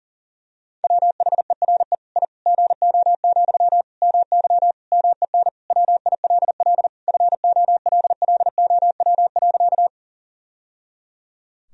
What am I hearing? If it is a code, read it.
Morse code, heard as a keyed tone, about 40 wpm, 700 Hz. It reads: WHERE I GO, MY MEN WILL FOLLOW.
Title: Sentences from Top 1000 Words - Review Part 1 - 40wpm